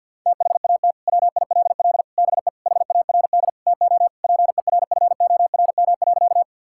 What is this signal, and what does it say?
Morse code, heard as a keyed tone, about 46 wpm, 700 Hz. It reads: THAT WILL BE HARD TO PERFORM.
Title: Sentences from Top 1000 Words - Review Part 5 - 50wpm